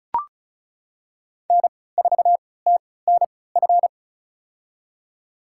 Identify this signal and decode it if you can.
Morse code, heard as a keyed tone, about 34 wpm, spaced as for 20 wpm, 700 Hz. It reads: N4TNF
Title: USA Call Signs - 5 Characters - 3x Character Spacing - 35wpm